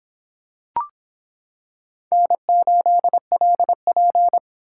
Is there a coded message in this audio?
Yes: Morse code from 2.1 to 4.4 s, heard as a keyed tone, about 25 wpm, 700 Hz. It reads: N8LP